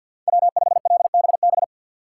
Morse code, heard as a keyed tone, about 48 wpm, 700 Hz. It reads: W5LBB
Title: USA Call Signs - 5 Characters - 4x Character Spacing - 50wpm